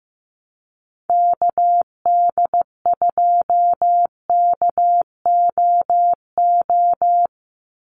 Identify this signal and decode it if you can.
Morse code, heard as a keyed tone, about 15 wpm, 700 Hz. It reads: KD2KOO
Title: USA Call Signs 15wpm